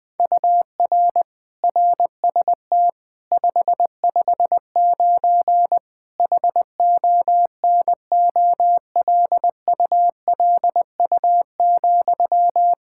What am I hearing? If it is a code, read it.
Morse code, heard as a keyed tone, about 20 wpm, 700 Hz. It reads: UR RST 559 HONOLULU,
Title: Practice QSO - Exchange 1 - Beginning - 20wpm